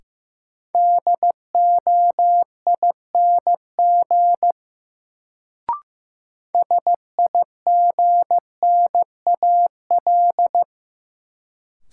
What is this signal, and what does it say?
Morse code, heard as a keyed tone, about 15 wpm, 700 Hz. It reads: DOING SIGNAL
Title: Sentences from Top 1400 Words - New Word Drill Alt - 15wpm